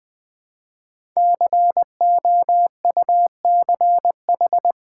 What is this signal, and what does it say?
Morse code, heard as a keyed tone, about 20 wpm, 700 Hz. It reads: COUCH
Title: Easy Words with Hints - 20wpm